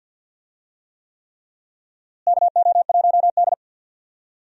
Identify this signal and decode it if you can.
Morse code, heard as a keyed tone, about 49 wpm, 700 Hz. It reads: KO1D
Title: USA Call Signs - 4 Characters - Repeated 3x - 50wpm